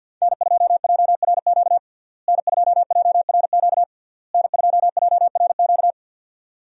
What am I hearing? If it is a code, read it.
Morse code, heard as a keyed tone, about 52 wpm, 700 Hz. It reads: N2JRQ N2JRQ N2JRQ
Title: USA Call Signs - 5 Characters - Repeated 3x - 50wpm